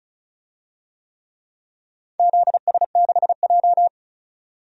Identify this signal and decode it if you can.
Morse code, heard as a keyed tone, about 34 wpm, 700 Hz. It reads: ZS6J
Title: DXCC-100 Call Signs - 4 Characters - 35wpm